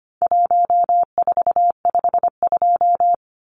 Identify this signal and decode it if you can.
Morse code, heard as a keyed tone, about 25 wpm, 700 Hz. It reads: 1452
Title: Numbers - Contest Style - 4-Digit 25wpm